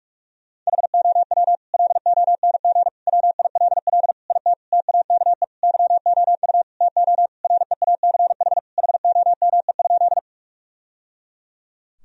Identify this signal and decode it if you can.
Morse code, heard as a keyed tone, about 42 wpm, 700 Hz. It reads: HOW LONG WILL IT TAKE YOU TO REACH HOME?
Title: Sentences from Top 1000 Words - Review Part 2 - 45wpm